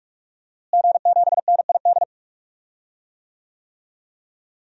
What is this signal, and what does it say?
Morse code, heard as a keyed tone, about 46 wpm, 700 Hz. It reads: G7NID
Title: DXCC-100 Call Signs - 5 Characters - Repeated 3x - 45wpm